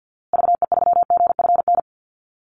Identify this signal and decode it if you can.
Morse code, heard as a keyed tone, about 50 wpm, 700 Hz. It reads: VE3GFN
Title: Call Signs Repeated 2x - 50wpm